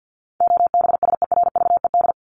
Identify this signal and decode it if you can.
Morse code, heard as a keyed tone, about 50 wpm, 700 Hz. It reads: OBSERVED
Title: ED Suffix - Top 100 Words with Base Words 50wpm